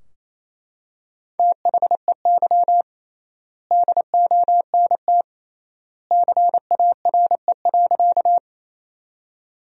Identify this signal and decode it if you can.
Morse code, heard as a keyed tone, about 28 wpm, 700 Hz. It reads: THEY DONT CARE.
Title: 3x Word-Spacing - Sentences from Top 300 Words - 28wpm